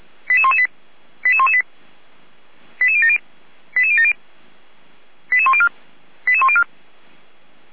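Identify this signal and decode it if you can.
Five-tone selective call (ZVEI), heard as quick runs of five tones, tones of about 70 ms. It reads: 88118 88118 88070 88070 88104 88104
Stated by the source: Five-tone ZVEI selective call received on SDR.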